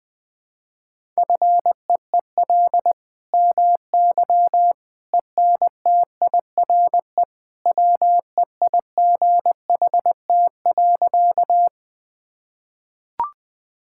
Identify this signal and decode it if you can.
Morse code, heard as a keyed tone, about 20 wpm, 700 Hz. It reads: FEEL MY ENTIRE WEIGHT.
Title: Sentences from Top 1000 Words - Review Part 5 - 20wpm